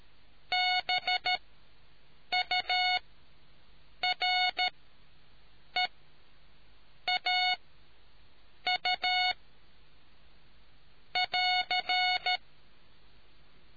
Morse code, heard as a keyed tone, about 13 wpm, spaced as for 6 wpm, 2200 Hz. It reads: BUREAU <AR>